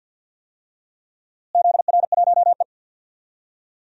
Morse code, heard as a keyed tone, about 52 wpm, 700 Hz. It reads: ZR1E
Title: DXCC-100 Call Signs - 4 Characters - Repeated 2x - 50wpm